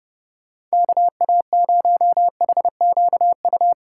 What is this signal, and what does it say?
Morse code, heard as a keyed tone, about 29 wpm, 700 Hz. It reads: KA0HQU